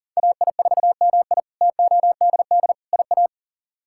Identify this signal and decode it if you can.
Morse code, heard as a keyed tone, about 40 wpm, 700 Hz. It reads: AI4MI TODD IA
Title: K1USN SST QSO Practice - 40wpm